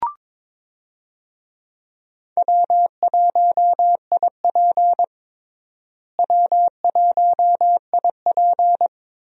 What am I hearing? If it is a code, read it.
Morse code, heard as a keyed tone, about 22 wpm, 700 Hz. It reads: W1IP W1IP
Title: USA Call Signs - 4 Characters - Repeated 2x - 22wpm